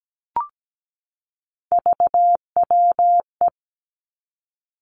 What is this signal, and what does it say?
Morse code, heard as a keyed tone, about 17 wpm, 700 Hz. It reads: VWE
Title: Three Challenging Letter-Number Combinations 17wpm